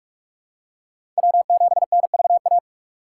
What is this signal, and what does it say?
Morse code, heard as a keyed tone, about 44 wpm, 700 Hz. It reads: W7NVA
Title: USA Call Signs - 8x Character Spacing - 45wpm